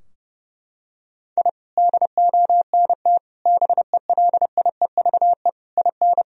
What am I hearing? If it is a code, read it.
Morse code, heard as a keyed tone, about 32 wpm, 700 Hz. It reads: I DONT BELIEVE IN
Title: Sentences from Top 1000 Words - Review Part 2 - 30wpm